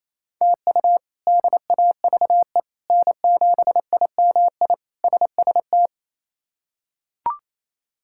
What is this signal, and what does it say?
Morse code, heard as a keyed tone, about 27 wpm, 700 Hz. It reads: TU DAVE N7IMI SST
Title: K1USN SST QSO Practice - 28wpm